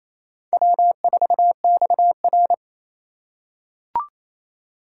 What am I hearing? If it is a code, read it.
Morse code, heard as a keyed tone, about 28 wpm, 700 Hz. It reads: W4XR
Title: USA Call Signs - 4x Character Spacing - 28wpm